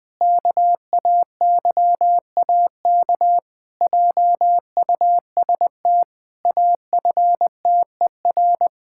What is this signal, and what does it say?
Morse code, heard as a keyed tone, about 19 wpm, 700 Hz. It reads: KAYAK JUST AFTER
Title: Pangram Challenge v2 - 20wpm